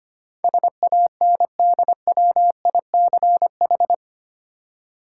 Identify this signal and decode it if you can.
Morse code, heard as a keyed tone, about 25 wpm, 700 Hz. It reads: SANDWICH
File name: practice / 25wpm / Easy Words with Hints - 25wpm